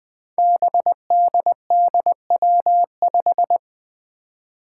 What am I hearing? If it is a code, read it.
Morse code, heard as a keyed tone, about 20 wpm, 700 Hz. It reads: BDDW5
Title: Five Challenging Letter-Number Combinations 20wpm